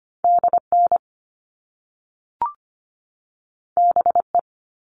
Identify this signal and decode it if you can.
Morse code, heard as a keyed tone, about 25 wpm, 700 Hz. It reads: DN BE
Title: All Cut Numbers - 2-Digit 25wpm